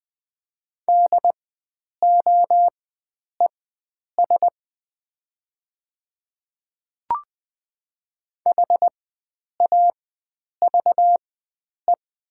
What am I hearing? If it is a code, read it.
Morse code, heard as a keyed tone, about 20 wpm, spaced as for 9 wpm, 700 Hz. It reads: DOES HAVE
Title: 4x Character Spacing - Top 300 Words - 20wpm